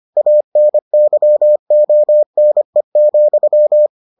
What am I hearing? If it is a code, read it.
Morse code, heard as a keyed tone, about 25 wpm, 600 Hz. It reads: ANYONE,